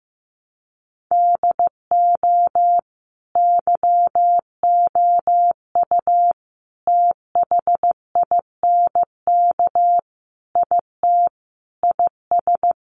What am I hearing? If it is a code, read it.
Morse code, heard as a keyed tone, about 15 wpm, 700 Hz. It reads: DO YOU THINK IT IS
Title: Sentences from Top 800 Words 15wpm